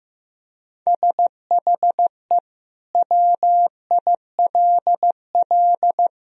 Morse code, heard as a keyed tone, about 15 wpm, 700 Hz. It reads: SHE WILL
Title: Sentences from Top 1000 Words - Review Part 3 - 15wpm